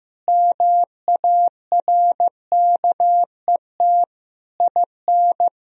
Morse code, heard as a keyed tone, about 15 wpm, 700 Hz. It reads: MARKET IN
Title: Sentences from Top 1000 Words - Review Part 3 - 15wpm